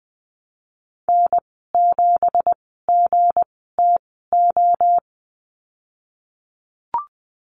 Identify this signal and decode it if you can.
Morse code, heard as a keyed tone, about 20 wpm, spaced as for 15 wpm, 700 Hz. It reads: N7GTO
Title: 193 Lesson 31g - Letter v - Call Signs 20wpm